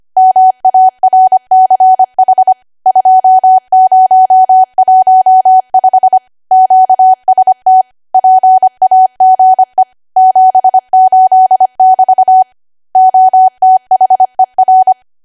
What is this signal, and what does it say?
Morse code, heard as a keyed tone, about 25 wpm, 750 Hz. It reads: MARCH 2015 QST PAGE 78<BT> OTHER
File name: Archive / 25 WPM / 25WPM